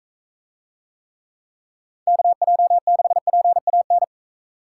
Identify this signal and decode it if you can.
Morse code, heard as a keyed tone, about 40 wpm, 700 Hz. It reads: KJ6PAN